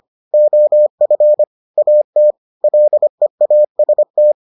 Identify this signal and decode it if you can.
Morse code, heard as a keyed tone, about 25 wpm, 600 Hz. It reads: OF AT LEAST